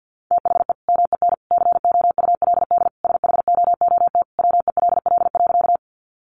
Morse code, heard as a keyed tone, about 50 wpm, 700 Hz. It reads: THE MEN COULD SHOOT WELL.
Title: Sentences from Top 1200 Words 50wpm